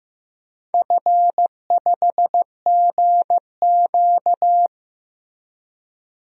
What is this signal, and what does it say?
Morse code, heard as a keyed tone, about 15 wpm, 700 Hz. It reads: F5GQ
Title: DXCC-100 Call Signs - 4 Characters - Repeated 3x - 15wpm